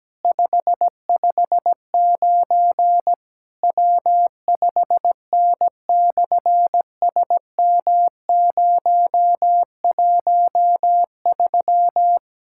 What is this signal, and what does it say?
Morse code, heard as a keyed tone, about 17 wpm, 700 Hz. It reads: 559 W5N/SM013
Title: SOTA QSO Practice 17wpm